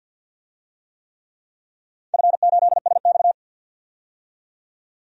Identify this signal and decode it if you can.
Morse code, heard as a keyed tone, about 53 wpm, 700 Hz. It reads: F8SX
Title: DXCC-100 Call Signs - 4 Characters - 50wpm